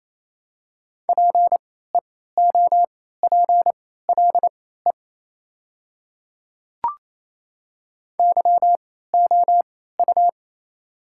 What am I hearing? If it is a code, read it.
Morse code, heard as a keyed tone, about 28 wpm, spaced as for 16 wpm, 700 Hz. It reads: PEOPLE YOU
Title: 3x Character Spacing - Top 200 Words - 28wpm